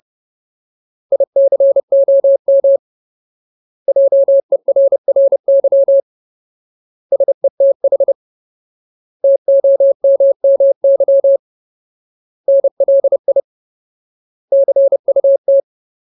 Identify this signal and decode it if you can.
Morse code, heard as a keyed tone, about 30 wpm, 550 Hz. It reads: ICOM JERRY SETH TOMMY NLI CUT
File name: Ultimate Word List / 30WPM Ultimate